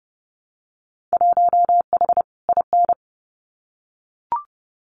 Morse code, heard as a keyed tone, about 30 wpm, 700 Hz. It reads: 1H IN
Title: Winter Field Day Exchanges 30wpm